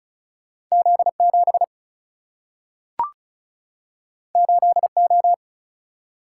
Morse code, heard as a keyed tone, about 35 wpm, 700 Hz. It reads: Z7 8O
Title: Single-Letter Single-Number 35wpm